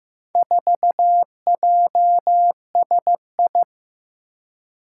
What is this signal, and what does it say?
Morse code, heard as a keyed tone, about 15 wpm, 700 Hz. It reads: 4JSI
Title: Four Challenging Letter-Number Combinations 15wpm